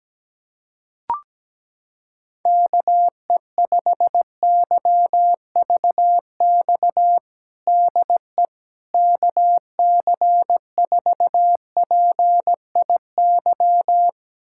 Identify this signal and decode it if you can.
Morse code, heard as a keyed tone, about 17 wpm, 700 Hz. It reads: KE5YVX DE KC4PIY
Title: USA Callsign DE Callsign - 6 Characters - 17wpm